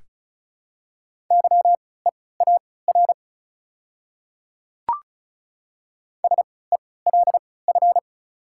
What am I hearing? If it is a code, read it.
Morse code, heard as a keyed tone, about 36 wpm, spaced as for 20 wpm, 700 Hz. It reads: YEAR SELF
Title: 3x Character Spacing - Top 300 Words - 35wpm